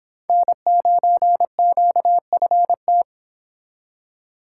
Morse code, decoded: N9QFT

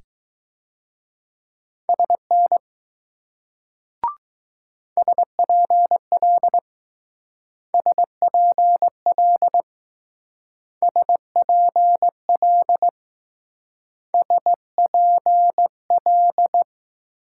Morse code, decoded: SN SPL SPL SPL SPL

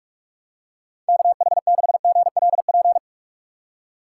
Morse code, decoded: KH6GLP

